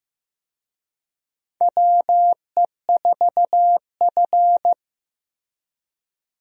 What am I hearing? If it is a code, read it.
Morse code, heard as a keyed tone, about 15 wpm, 700 Hz. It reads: WE4F
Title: USA Call Signs - 4 Characters - Repeated 3x - 15wpm